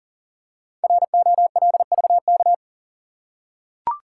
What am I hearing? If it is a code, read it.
Morse code, heard as a keyed tone, about 39 wpm, 700 Hz. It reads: ROLVK
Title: Code Groups - 5 Letter:Numbers - 40wpm